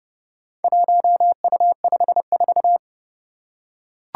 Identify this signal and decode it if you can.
Morse code, heard as a keyed tone, about 30 wpm, 700 Hz. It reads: 1U54
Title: Four Challenging Letter-Number Combinations 30wpm